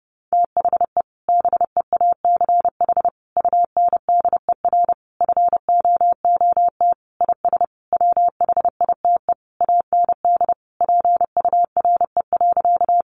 Morse code, heard as a keyed tone, about 30 wpm, 700 Hz. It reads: THE BEACH UNDER FOOT IS WHITE AND PURE.
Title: Sentences from Top 1500 Words 30wpm